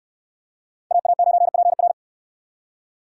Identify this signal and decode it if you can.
Morse code, heard as a keyed tone, about 68 wpm, 700 Hz. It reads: AA9PR